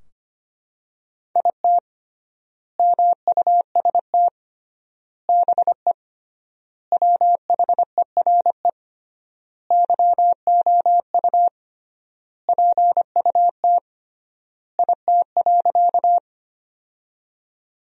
Morse code, heard as a keyed tone, about 25 wpm, 700 Hz. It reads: IT MUST BE WHERE YOU PUT IT.
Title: 3x Word-Spacing - Sentences from Top 200 Words - 25wpm